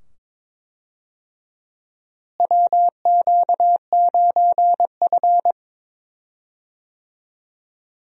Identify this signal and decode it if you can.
Morse code, heard as a keyed tone, about 22 wpm, 700 Hz. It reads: WQ9F
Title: USA Call Signs - 4 Characters - Repeated 2x - 22wpm